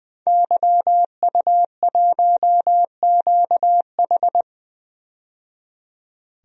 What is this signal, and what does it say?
Morse code, heard as a keyed tone, about 21 wpm, 700 Hz. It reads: YU1QH
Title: DXCC-100 Call Signs - 5 Characters - 20wpm